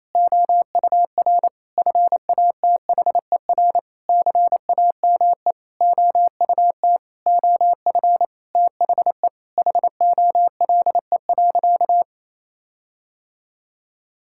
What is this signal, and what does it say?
Morse code, heard as a keyed tone, about 28 wpm, 700 Hz. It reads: OUR FATHER CAME OUT OF THE HOLE.
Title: Sentences from Top 1600 Words 28wpm